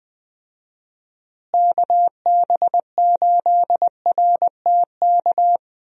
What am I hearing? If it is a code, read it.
Morse code, heard as a keyed tone, about 21 wpm, 700 Hz. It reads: KB8RTK
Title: USA Call Signs - 6 Characters - Repeated 2x - 20wpm